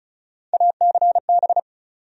Morse code, heard as a keyed tone, about 34 wpm, 700 Hz. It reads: ACB
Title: USA Call Sign Postfixes - 3 Letters 35wpm